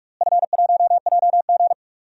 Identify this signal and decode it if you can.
Morse code, heard as a keyed tone, about 46 wpm, 700 Hz. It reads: F1JG
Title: DXCC-100 Call Signs - 4 Characters - Repeated 3x - 45wpm